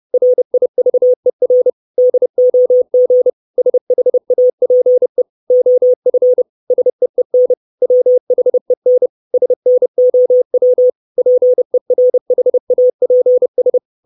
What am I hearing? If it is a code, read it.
Morse code, heard as a keyed tone, about 30 wpm, 500 Hz. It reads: RIVER DOG SHAPE OF SEEN WHEN SNOW PERHAPS
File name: 632 Words 30WPM